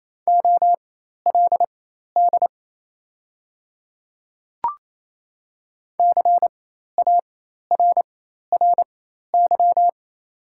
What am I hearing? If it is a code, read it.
Morse code, heard as a keyed tone, about 28 wpm, spaced as for 13 wpm, 700 Hz. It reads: OLD CARRY
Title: 4x Character Spacing - Top 300 Words - 28wpm